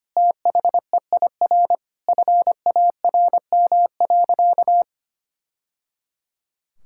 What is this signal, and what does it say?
Morse code, heard as a keyed tone, about 25 wpm, 700 Hz. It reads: THEIR FARM.
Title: Sentences from Top 1000 Words - Review Part 6 - 25wpm